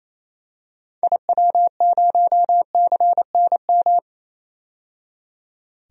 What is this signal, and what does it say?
Morse code, heard as a keyed tone, about 27 wpm, 700 Hz. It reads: IW0CNM
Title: DXCC-100 Call Signs - 6 Characters - 28wpm